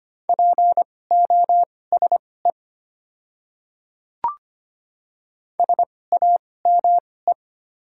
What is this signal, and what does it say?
Morse code, heard as a keyed tone, about 25 wpm, spaced as for 18 wpm, 700 Hz. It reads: POSE SAME
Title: 2x Character Spacing - Top 500 Words - 25wpm